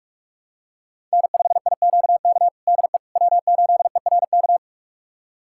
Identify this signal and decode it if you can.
Morse code, heard as a keyed tone, about 42 wpm, 700 Hz. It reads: N5IQK DE W8ERK
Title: USA Callsign DE Callsign - 5 Characters - 45wpm